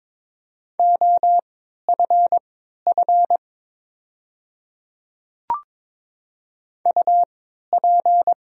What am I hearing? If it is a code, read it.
Morse code, heard as a keyed tone, about 22 wpm, spaced as for 12 wpm, 690 Hz. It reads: OFF UP
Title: 3x Character Spacing - Top 200 Words - 22wpm